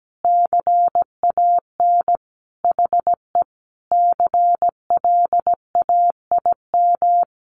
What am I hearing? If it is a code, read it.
Morse code, heard as a keyed tone, about 17 wpm, 700 Hz. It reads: CAN HE CLAIM
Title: Sentences from Top 1500 Words 17wpm